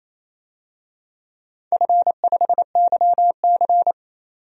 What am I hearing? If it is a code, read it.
Morse code, heard as a keyed tone, about 28 wpm, 700 Hz. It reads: F5YC
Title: DXCC-100 Call Signs - 4 Characters - 28wpm